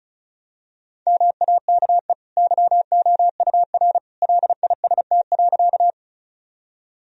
Morse code, heard as a keyed tone, about 34 wpm, 700 Hz. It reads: MAKE YOUR LIST.
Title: Sentences from Top 1000 Words - Review Part 2 - 35wpm